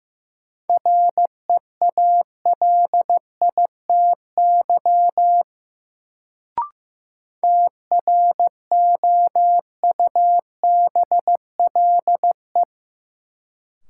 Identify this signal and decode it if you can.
Morse code, heard as a keyed tone, about 15 wpm, 700 Hz. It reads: REALITY TROUBLE
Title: Sentences from Top 1200 Words - New Word Drill Alt - 15wpm